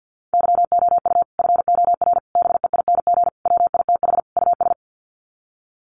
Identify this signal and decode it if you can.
Morse code, heard as a keyed tone, about 50 wpm, 700 Hz. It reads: YOU FOR BEING WITH US
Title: Sets of 5 Words 50wpm